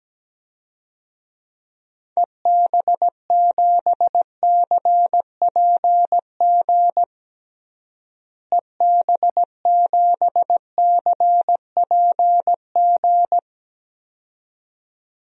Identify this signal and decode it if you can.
Morse code, heard as a keyed tone, about 17 wpm, 700 Hz. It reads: EB7CPG EB7CPG
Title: DXCC-100 Call Signs - 6 Characters - Repeated 2x - 17wpm